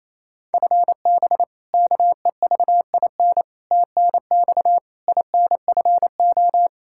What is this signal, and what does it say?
Morse code, heard as a keyed tone, about 28 wpm, 700 Hz. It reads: FB KEVIN TNX INFO
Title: Practice QSO - Exchange 3 - More Info - 28wpm